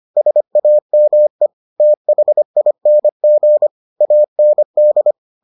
Morse code, heard as a keyed tone, about 25 wpm, 600 Hz. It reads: SAME THING AND